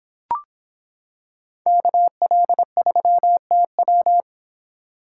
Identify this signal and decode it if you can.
Morse code, heard as a keyed tone, about 26 wpm, 700 Hz. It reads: KL3TW